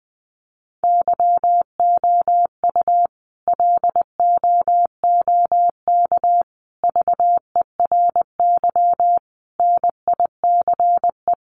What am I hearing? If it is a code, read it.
Morse code, heard as a keyed tone, about 20 wpm, 700 Hz. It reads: YOU LOOK VERY NICE
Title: Sentences from Top 1500 Words 20wpm